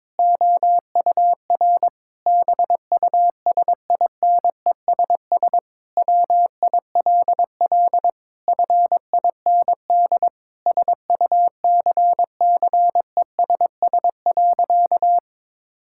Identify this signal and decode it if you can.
Morse code, heard as a keyed tone, about 22 wpm, 700 Hz. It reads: OUR BUSINESS WILL FIND SUCCESS.
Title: Sentences from Top 1000 Words - Review Part 5 - 22wpm